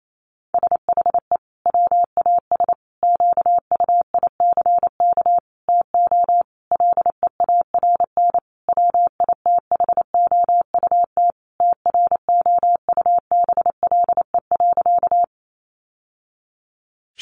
Morse code, heard as a keyed tone, about 28 wpm, 700 Hz. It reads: SHE WAS QUICK TO LEARN WITHOUT TROUBLE.
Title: Sentences from Top 1200 Words 28wpm